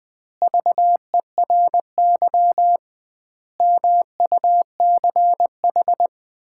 Morse code, heard as a keyed tone, about 21 wpm, 700 Hz. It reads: VERY MUCH
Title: 2x Word-Spacing - Sets of 5 Words - 20wpm